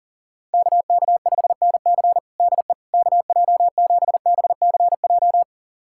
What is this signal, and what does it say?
Morse code, heard as a keyed tone, about 40 wpm, 700 Hz. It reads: KK5NC DE KJ7BCJ
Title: USA Callsign DE Callsign 40wpm